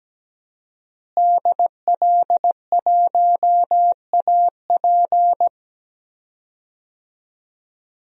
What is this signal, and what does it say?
Morse code, heard as a keyed tone, about 17 wpm, 700 Hz. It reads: DL1AP
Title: DXCC-100 Call Signs - 5 Characters - Repeated 2x - 17wpm